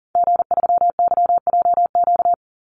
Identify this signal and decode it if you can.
Morse code, heard as a keyed tone, about 40 wpm, 700 Hz. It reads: G3YJQ
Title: Call Signs Repeated 2x - 40wpm